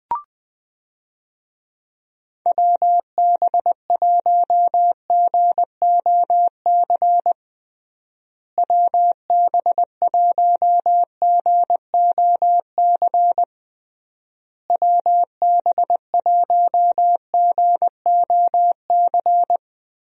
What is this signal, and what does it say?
Morse code, heard as a keyed tone, about 20 wpm, 700 Hz. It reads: WB1GOC WB1GOC WB1GOC
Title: USA Call Signs - 6 Characters - Repeated 3x - 20wpm